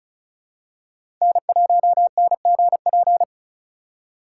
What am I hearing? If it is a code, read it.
Morse code, heard as a keyed tone, about 36 wpm, 700 Hz. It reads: N1NGP
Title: USA Call Signs - 5 Characters - 8x Character Spacing - 35wpm